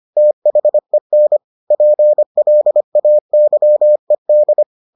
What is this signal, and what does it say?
Morse code, heard as a keyed tone, about 25 wpm, 600 Hz. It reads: THEN PLAYED